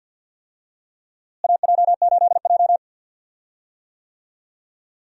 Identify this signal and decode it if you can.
Morse code, heard as a keyed tone, about 46 wpm, 700 Hz. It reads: AJ8J
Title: USA Call Signs - 4 Characters - Repeated 3x - 50wpm